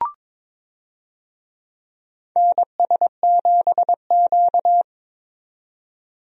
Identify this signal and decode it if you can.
Morse code, heard as a keyed tone, about 22 wpm, 700 Hz. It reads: NS7Q
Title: USA Call Signs - 4 Characters - 22wpm